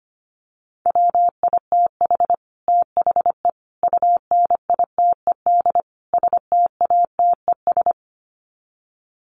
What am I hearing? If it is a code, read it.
Morse code, heard as a keyed tone, about 25 wpm, 700 Hz. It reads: WITH THE UNITED STATES
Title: Sets of 4 Words 25wpm